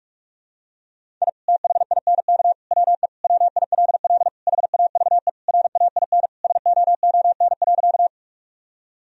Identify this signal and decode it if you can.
Morse code, heard as a keyed tone, about 45 wpm, 700 Hz. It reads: I THINK WE WILL HAVE RAIN SOON.
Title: Sentences from Top 1000 Words - Review Part 1 - 45wpm